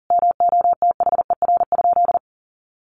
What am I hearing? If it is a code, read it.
Morse code, heard as a keyed tone, about 40 wpm, 700 Hz. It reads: MOTHER?
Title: Sentences from Top 1300 Words 40wpm